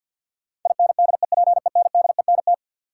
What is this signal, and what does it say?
Morse code, heard as a keyed tone, about 52 wpm, 700 Hz. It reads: INDEPENDENT
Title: Sentences from Top 900 Words - New Word Drill - 50wpm